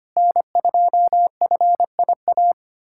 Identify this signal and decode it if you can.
Morse code, heard as a keyed tone, about 25 wpm, 700 Hz. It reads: N2FIA
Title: USA Callsign DE Callsign - 5 Characters - 25wpm